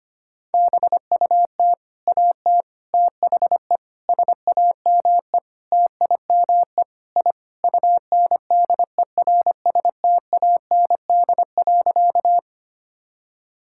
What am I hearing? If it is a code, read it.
Morse code, heard as a keyed tone, about 25 wpm, 700 Hz. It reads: BUT AT THE SAME TIME I UNDERSTAND.
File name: practice / 25wpm / Sentences with 4-Word Phrases - 25wpm